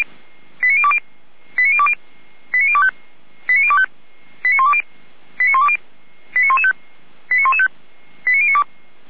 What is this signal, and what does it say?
Five-tone selective call (ZVEI), heard as quick runs of five tones, tones of about 70 ms. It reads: 89020 89020 89025 89025 89120 89120 89115 89115 89092